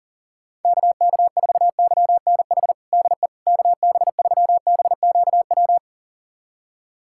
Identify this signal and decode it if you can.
Morse code, heard as a keyed tone, about 40 wpm, 700 Hz. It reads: KK4YNH DE KB3BQW
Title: USA Callsign DE Callsign - 6 Characters - 40wpm